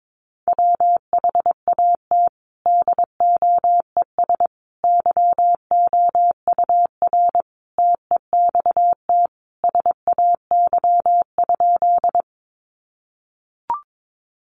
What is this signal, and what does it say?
Morse code, heard as a keyed tone, about 22 wpm, 700 Hz. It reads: WHAT DOES YOUR TEXT SAY?
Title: Sentences from Top 1200 Words 22wpm